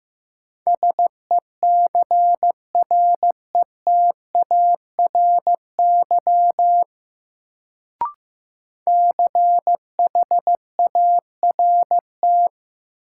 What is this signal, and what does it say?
Morse code, heard as a keyed tone, about 15 wpm, 700 Hz. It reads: SECRETARY CHART
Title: Sentences from Top 1300 Words - New Word Drill - 15wpm